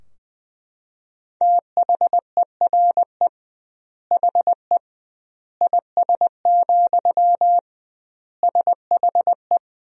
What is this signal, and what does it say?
Morse code, heard as a keyed tone, about 20 wpm, 700 Hz. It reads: THERE HE IS, SHE
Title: 2x Word-Spacing - Sentences from Top 100 Words - 20wpm